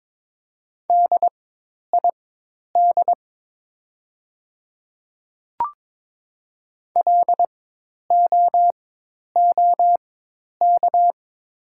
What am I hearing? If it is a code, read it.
Morse code, heard as a keyed tone, about 22 wpm, spaced as for 10 wpm, 700 Hz. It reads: DID LOOK
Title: 4x Character Spacing - Top 500 Words - 22wpm